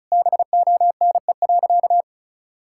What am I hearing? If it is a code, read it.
Morse code, heard as a keyed tone, about 34 wpm, 700 Hz. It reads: BONE.
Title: Sentences from Top 1600 Words 35wpm